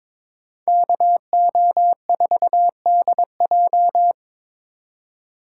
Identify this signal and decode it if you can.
Morse code, heard as a keyed tone, about 26 wpm, 700 Hz. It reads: KO4DJ
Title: Speed Racing - USA 2x2 Call Signs 22wpm